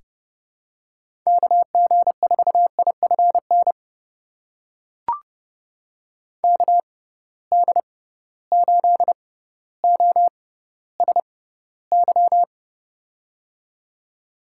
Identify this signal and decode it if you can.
Morse code, heard as a keyed tone, about 30 wpm, spaced as for 15 wpm, 700 Hz. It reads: KG4IFN KD8OSY
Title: USA Call Signs - 6 Characters - 6x Character Spacing - 30wpm